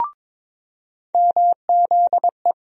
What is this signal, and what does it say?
Morse code, heard as a keyed tone, about 22 wpm, 700 Hz. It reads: MZE
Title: USA Call Sign Postfixes - 3 Letters 22wpm